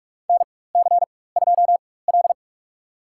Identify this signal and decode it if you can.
Morse code, heard as a keyed tone, about 47 wpm, spaced as for 21 wpm, 700 Hz. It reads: NC2L
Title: USA Call Signs - 4 Characters - 4x Character Spacing - 45wpm